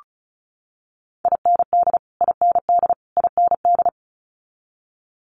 Morse code, heard as a keyed tone, about 34 wpm, 700 Hz. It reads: IND IND IND
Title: Multi-letter Phonemes Repeated 3x 35wpm